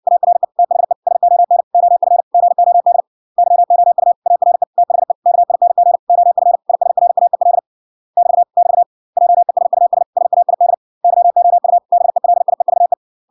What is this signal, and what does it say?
Morse code, heard as a keyed tone, about 58 wpm, 700 Hz. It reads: ARE THE SON OF GOD YOU ARE THE KING OF ISRAEL <BT> <BT> JESUS SAID YOU BELIEVE